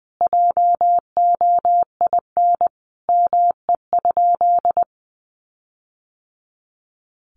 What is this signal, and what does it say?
Morse code, heard as a keyed tone, about 20 wpm, 700 Hz. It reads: JOIN ME?
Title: Sentences from Top 1400 Words 20wpm